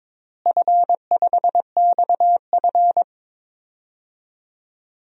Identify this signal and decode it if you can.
Morse code, heard as a keyed tone, about 22 wpm, 700 Hz. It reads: F5XF
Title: DXCC-100 Call Signs - 4 Characters - Repeated 3x - 22wpm